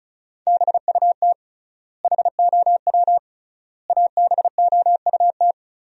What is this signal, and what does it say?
Morse code, heard as a keyed tone, about 34 wpm, 700 Hz. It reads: BUT HOW ABOUT
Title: 3x Word-Spacing - Sentences from Top 100 Words - 35wpm